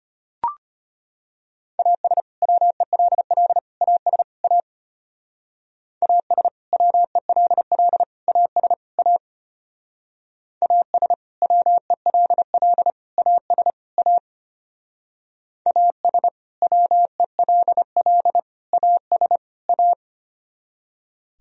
Morse code, heard as a keyed tone, about 33 wpm, 700 Hz. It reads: AS WELL AS A AS WELL AS A AS WELL AS A AS WELL AS A